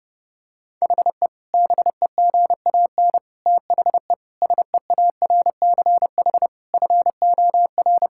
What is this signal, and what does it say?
Morse code, heard as a keyed tone, about 32 wpm, 700 Hz. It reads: HE BEGAN THE SEARCH FOR
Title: Sentences from Top 1000 Words - Review Part 5 - 30wpm